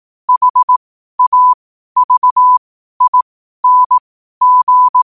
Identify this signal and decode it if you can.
Morse code, heard as a keyed tone, about 18 wpm, spaced as for 12 wpm, 1000 Hz. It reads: HAVING
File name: qotdfiles / 12wpm